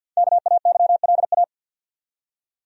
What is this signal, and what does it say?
Morse code, heard as a keyed tone, about 52 wpm, 700 Hz. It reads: KAYLA
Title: USA Common Names - 50wpm